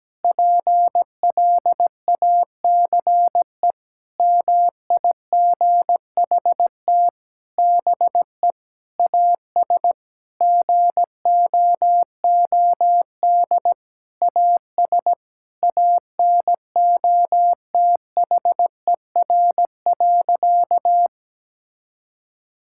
Morse code, heard as a keyed tone, about 17 wpm, 700 Hz. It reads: PLACE MIGHT BE AS GOOD AS ANOTHER.
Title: Sentences from Top 1600 Words 17wpm